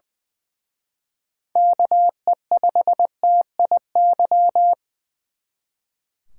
Morse code, heard as a keyed tone, about 24 wpm, 700 Hz. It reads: KE5TIY
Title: Speed Racing - USA Call Signs 20wpm